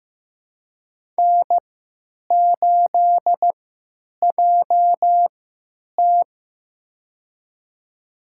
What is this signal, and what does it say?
Morse code, heard as a keyed tone, about 15 wpm, spaced as for 9 wpm, 700 Hz. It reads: N8JT
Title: USA Call Signs - 4 Characters - 3x Character Spacing - 15wpm